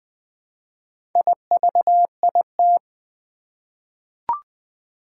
Morse code, heard as a keyed tone, about 20 wpm, 700 Hz. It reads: IVIT
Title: Four Challenging Letter-Number Combinations 20wpm